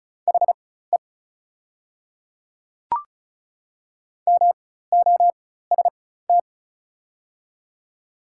Morse code, heard as a keyed tone, about 36 wpm, spaced as for 16 wpm, 700 Hz. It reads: HE MOST
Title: 4x Character Spacing - Top 100 Words - 35wpm